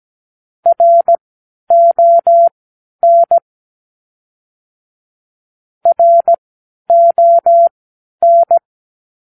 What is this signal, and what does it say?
Morse code, heard as a keyed tone, about 17 wpm, spaced as for 10 wpm, 680 Hz. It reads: RON RON
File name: k7qo / Practicewords